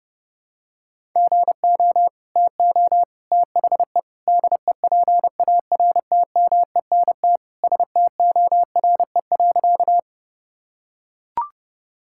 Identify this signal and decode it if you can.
Morse code, heard as a keyed tone, about 30 wpm, 700 Hz. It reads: GO TO THE DEPARTMENT STORE.
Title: Sentences from Top 1000 Words - Review Part 5 - 30wpm